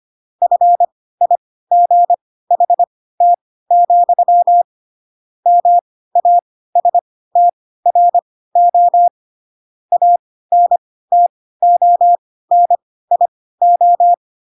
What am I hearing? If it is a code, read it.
Morse code, heard as a keyed tone, about 25 wpm, spaced as for 16 wpm, 700 Hz. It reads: FIGHT, MASTRO ANTONIO
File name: Pinocchio / Pinocchio ch2 10 25 ik1hge